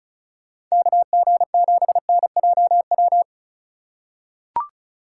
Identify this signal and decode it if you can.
Morse code, heard as a keyed tone, about 36 wpm, 700 Hz. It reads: KG7NJW
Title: USA Call Signs - 6x Character Spacing - 35wpm